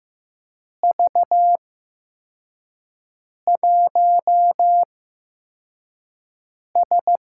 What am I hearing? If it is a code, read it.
Morse code, heard as a keyed tone, about 15 wpm, spaced as for 4 wpm, 700 Hz. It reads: V1S
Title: USA Call Signs - 4 Characters - 8x Character Spacing - 15wpm